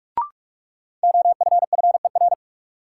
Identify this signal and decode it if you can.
Morse code, heard as a keyed tone, about 46 wpm, 700 Hz. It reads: OFFER
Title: Sentences from Top 900 Words - New Word Drill - 45wpm